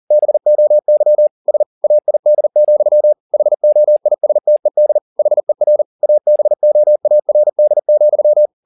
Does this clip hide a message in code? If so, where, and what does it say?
Yes: Morse code from 0.1 to 8.5 s, heard as a keyed tone, about 40 wpm, 600 Hz. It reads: BOY SAID, HOISTED HER ABOARD,